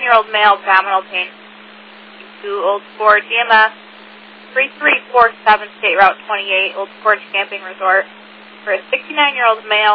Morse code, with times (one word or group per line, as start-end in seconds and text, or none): none